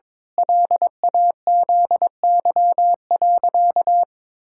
Morse code, heard as a keyed tone, about 22 wpm, 700 Hz. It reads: LAZY.
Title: Pangram Challenge v2 - 22wpm